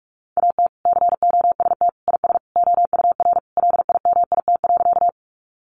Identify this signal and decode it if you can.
Morse code, heard as a keyed tone, about 44 wpm, 700 Hz. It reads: AT COST IS OUR LIMIT.
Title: Sentences from Top 1500 Words 45wpm